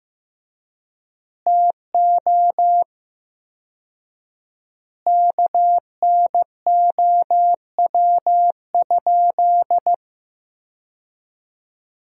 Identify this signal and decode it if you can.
Morse code, heard as a keyed tone, about 15 wpm, 700 Hz. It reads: TO KNOW?
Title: 4x Word-Spacing - Sentences from Top 200 Words - 15wpm